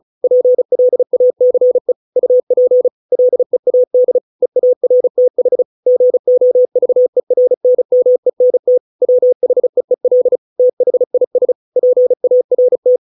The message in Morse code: PLACE UP LEAD EARTH GOVERNMENT WHEEL THIS PART

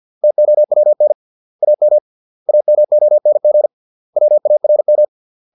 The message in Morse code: TOWN AM AMONG WARM

